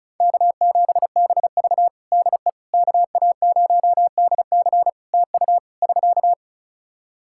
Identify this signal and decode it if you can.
Morse code, heard as a keyed tone, about 34 wpm, 700 Hz. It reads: K7BV DE KA0DC TU <SK>
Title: Practice QSO - Exchange 4 - Ending - 35wpm